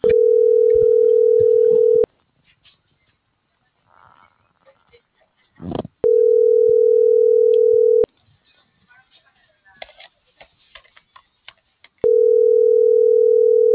Ringback tone (North America), 440 + 480 Hz.
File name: CENTRAL TERRESTRE / out ramon.lamorena.2024